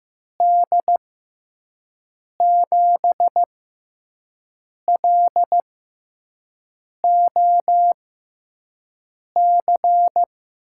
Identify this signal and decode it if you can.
Morse code, heard as a keyed tone, about 15 wpm, spaced as for 5 wpm, 700 Hz. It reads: D7LOC
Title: USA Call Signs - 6 Characters - 6x Character Spacing - 15wpm